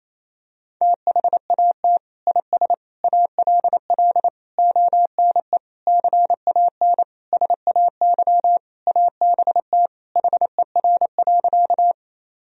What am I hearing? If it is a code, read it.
Morse code, heard as a keyed tone, about 28 wpm, 700 Hz. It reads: THAT IS ALL ONE CAN SAY ABT HER.
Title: Sentences from Top 200 Words and Abbreviations 28wpm